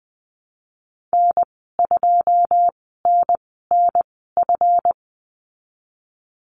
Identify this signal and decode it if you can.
Morse code, heard as a keyed tone, about 20 wpm, spaced as for 14 wpm, 700 Hz. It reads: N2NNF
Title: 161 Lesson 26g - Letter g - Call Signs 20wpm